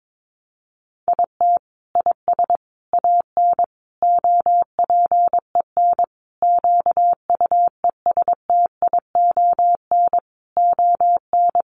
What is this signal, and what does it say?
Morse code, heard as a keyed tone, about 22 wpm, 700 Hz. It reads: IT IS AN OPEN QUESTION ON
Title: Sentences from Top 1300 Words 22wpm